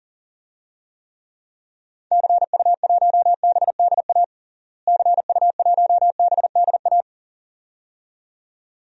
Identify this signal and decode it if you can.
Morse code, heard as a keyed tone, about 40 wpm, 700 Hz. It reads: CU1BDA CU1BDA
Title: DXCC-100 Call Signs - 6 Characters - Repeated 2x - 40wpm